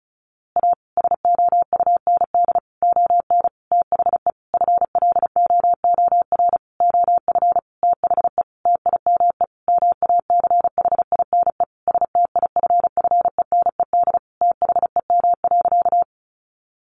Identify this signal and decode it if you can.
Morse code, heard as a keyed tone, about 34 wpm, 700 Hz. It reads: A SOUND ON THE FLOOR OF THE TIME MACHINE STIFFENED THEM.
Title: A Sound Of Thunder 35wpm